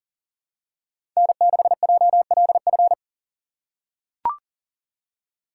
Morse code, heard as a keyed tone, about 40 wpm, 700 Hz. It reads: N6JLF
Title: USA Call Signs - 5 Characters - 2x Character Spacing - 40wpm